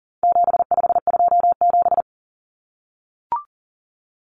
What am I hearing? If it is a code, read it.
Morse code, heard as a keyed tone, about 40 wpm, 700 Hz. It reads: 7527
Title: Numbers - 4-Digit 40wpm